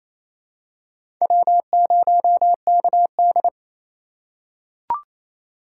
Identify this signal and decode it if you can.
Morse code, heard as a keyed tone, about 28 wpm, 700 Hz. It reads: W0KD
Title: USA Call Signs - 4 Characters - 2x Character Spacing - 28wpm